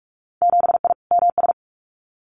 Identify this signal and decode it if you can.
Morse code, heard as a keyed tone, about 44 wpm, 690 Hz. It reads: 7I MS
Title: Winter Field Day Exchanges Encore 45wpm